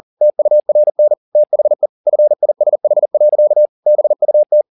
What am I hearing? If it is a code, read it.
Morse code, heard as a keyed tone, about 40 wpm, 600 Hz. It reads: TURN THE FISH. BUT